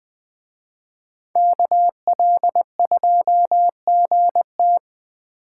Morse code, heard as a keyed tone, about 20 wpm, 700 Hz. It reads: KL2GT